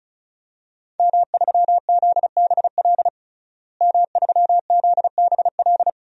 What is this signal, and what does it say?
Morse code, heard as a keyed tone, about 34 wpm, 700 Hz. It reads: M3ZBL M3ZBL
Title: DXCC-100 Call Signs - 5 Characters - Repeated 2x - 35wpm